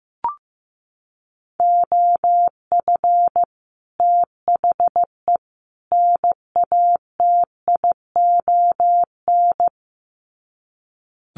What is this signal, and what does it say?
Morse code, heard as a keyed tone, about 15 wpm, 700 Hz. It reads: OF THE NATION
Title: Sets of 3 Words 15wpm